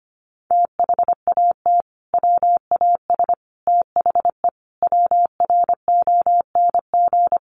Morse code, heard as a keyed tone, about 25 wpm, 700 Hz. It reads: THAT WAS THE WRONG
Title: Sentences from Top 900 Words 25wpm